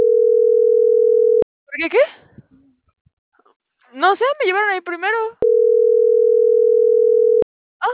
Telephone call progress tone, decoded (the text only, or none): ringback tone